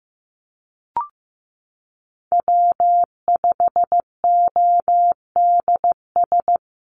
Morse code, heard as a keyed tone, about 15 wpm, 700 Hz. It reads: W5ODS
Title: NA SOTA Activator Callsigns 15wpm